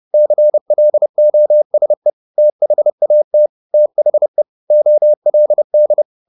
Morse code, heard as a keyed tone, about 30 wpm, 600 Hz. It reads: CLOSE THAT THE OLD
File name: the old man and the sea / the old man and the sea 1 30wpm